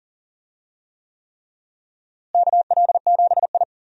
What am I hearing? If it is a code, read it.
Morse code, heard as a keyed tone, about 40 wpm, 700 Hz. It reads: KL7I
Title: DXCC-100 Call Signs - 4 Characters - Repeated 3x - 40wpm